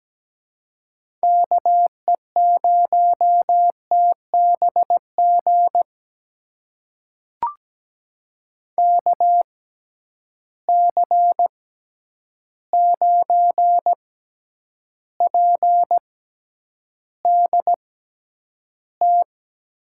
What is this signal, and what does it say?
Morse code, heard as a keyed tone, about 17 wpm, spaced as for 9 wpm, 700 Hz. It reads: KE0TBG KC9PDT
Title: USA Call Signs - 6 Characters - 6x Character Spacing - 17wpm